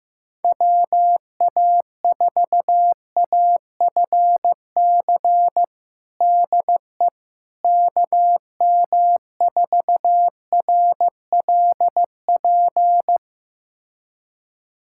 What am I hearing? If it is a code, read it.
Morse code, heard as a keyed tone, about 15 wpm, 700 Hz. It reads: WA4AFC DE KM4RLP
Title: USA Callsign DE Callsign - 6 Characters - 15wpm